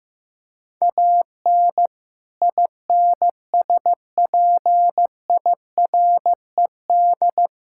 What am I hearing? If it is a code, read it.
Morse code, heard as a keyed tone, about 15 wpm, 700 Hz. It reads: AN INSPIRED
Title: Pangram Challenge v2 - 15wpm